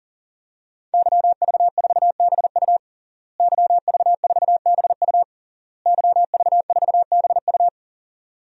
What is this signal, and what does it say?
Morse code, heard as a keyed tone, about 40 wpm, 700 Hz. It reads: YV4BU YV4BU YV4BU
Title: DXCC-100 Call Signs - 5 Characters - Repeated 3x - 40wpm